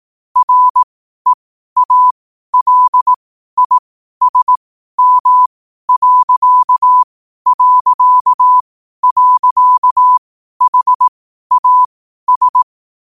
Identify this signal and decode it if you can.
Morse code, heard as a keyed tone, about 18 wpm, spaced as for 12 wpm, 1000 Hz. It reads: REALISM...HAS
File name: qotdfiles / 12wpm